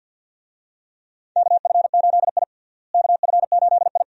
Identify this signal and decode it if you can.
Morse code, heard as a keyed tone, about 47 wpm, 700 Hz. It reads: KF8I KF8I
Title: USA Call Signs - 4 Characters - Repeated 2x - 50wpm